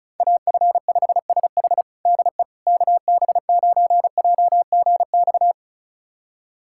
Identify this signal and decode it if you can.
Morse code, heard as a keyed tone, about 36 wpm, 700 Hz. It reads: AF5SH DE KB9JGX
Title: USA Callsign DE Callsign 35wpm